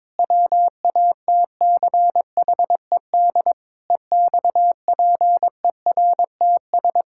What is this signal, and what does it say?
Morse code, heard as a keyed tone, about 22 wpm, 700 Hz. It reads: WATCHED EXPERTS
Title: Pangram Challenge v2 - 22wpm